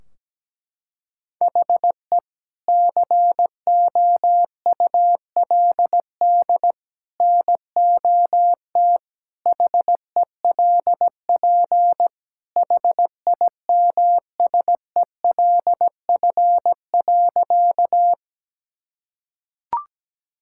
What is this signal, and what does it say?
Morse code, heard as a keyed tone, about 17 wpm, 700 Hz. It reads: HE COULD NOT HELP HIMSELF.
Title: Sentences from Top 1600 Words 17wpm